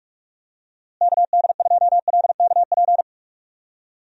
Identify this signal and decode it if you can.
Morse code, heard as a keyed tone, about 46 wpm, 700 Hz. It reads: KD2LKP
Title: USA Call Signs - 6 Characters - Repeated 3x - 45wpm